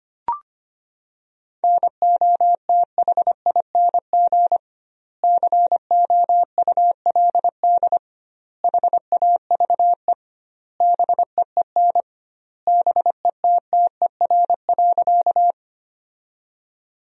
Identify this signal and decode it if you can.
Morse code, heard as a keyed tone, about 25 wpm, 700 Hz. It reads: NOTHING COULD HAVE BEEN BETTER.
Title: 2x Word-Spacing - Sentences from Top 500 Words - 25wpm